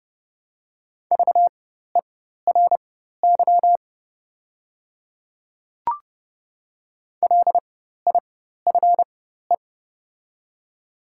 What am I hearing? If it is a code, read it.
Morse code, heard as a keyed tone, about 30 wpm, spaced as for 14 wpm, 700 Hz. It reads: VERY LIFE
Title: 4x Character Spacing - Top 300 Words - 30wpm